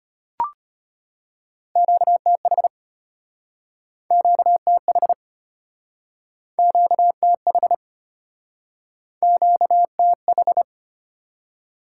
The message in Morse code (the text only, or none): QTH QTH QTH QTH